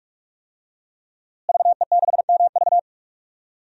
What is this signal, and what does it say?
Morse code, heard as a keyed tone, about 44 wpm, 700 Hz. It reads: VE6MV
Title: DXCC-100 Call Signs - 5 Characters - Repeated 3x - 45wpm